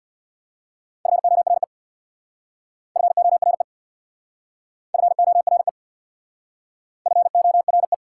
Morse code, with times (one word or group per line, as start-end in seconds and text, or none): 1.1-1.6 FORE
3.0-3.6 FORE
4.9-5.7 FORE
7.1-7.9 FORE